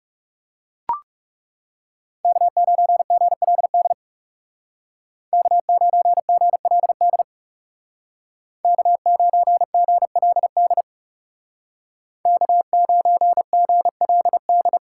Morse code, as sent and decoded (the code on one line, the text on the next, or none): -.- ----. --. .-.. -.. / -.- ----. --. .-.. -.. / -.- ----. --. .-.. -.. / -.- ----. --. .-.. -..
K9GLD K9GLD K9GLD K9GLD